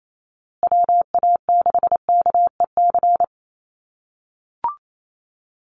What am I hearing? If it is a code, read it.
Morse code, heard as a keyed tone, about 28 wpm, 700 Hz. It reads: WA6KEC
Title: USA Call Signs Repeated 2x 28wpm